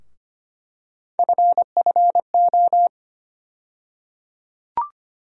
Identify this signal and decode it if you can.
Morse code, heard as a keyed tone, about 25 wpm, 700 Hz. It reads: FFO
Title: USA Call Sign Postfixes - 3 Letters 25wpm